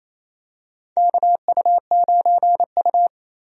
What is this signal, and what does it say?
Morse code, heard as a keyed tone, about 28 wpm, 700 Hz. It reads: KU9U
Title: USA Call Signs - 4 Characters - Repeated 3x - 28wpm